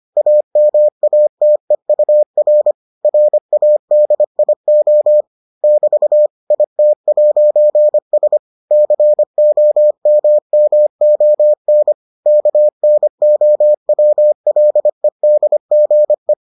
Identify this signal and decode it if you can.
Morse code, heard as a keyed tone, about 25 wpm, 600 Hz. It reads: AMATEUR RADIO <BT> IT'S COMMON KNOWLEDGE